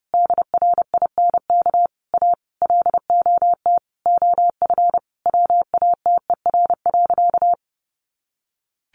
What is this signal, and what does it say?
Morse code, heard as a keyed tone, about 30 wpm, 700 Hz. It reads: DRINK A LOT OF WATER.
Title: Additional Easy Phrases 30wpm